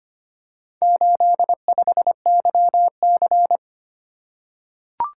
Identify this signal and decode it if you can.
Morse code, heard as a keyed tone, about 25 wpm, 700 Hz. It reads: 85YC
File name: practice / 25wpm / Four Letter-Number Combinations - 25wpm